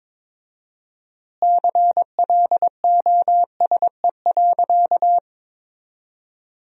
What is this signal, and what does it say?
Morse code, heard as a keyed tone, about 22 wpm, 700 Hz. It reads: CLOSE.
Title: 4x Word-Spacing - Sentences from Top 300 Words - 22wpm